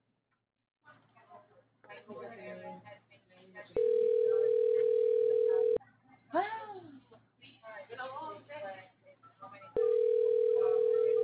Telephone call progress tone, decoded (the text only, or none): ringback tone